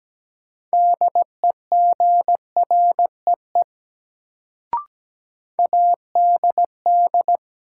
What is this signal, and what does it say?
Morse code, heard as a keyed tone, about 17 wpm, 700 Hz. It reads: DEGREE ADD
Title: Sentences from Top 1500 Words - New Word Drill Alt - 17wpm